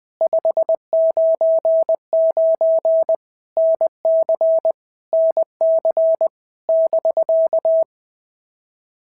Morse code, heard as a keyed tone, about 19 wpm, 650 Hz. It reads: 599 NC NC <BK>
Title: POTA QSO Practice - 20wpm